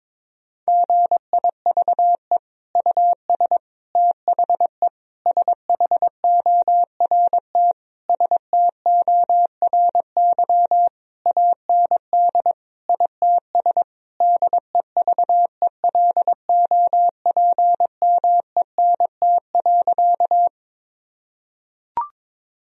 Morse code, heard as a keyed tone, about 22 wpm, 700 Hz. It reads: GIVE US THE SHORT STORY AND ITS DEVELOPMENT.
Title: Sentences from Top 1000 Words - Review Part 4 - 22wpm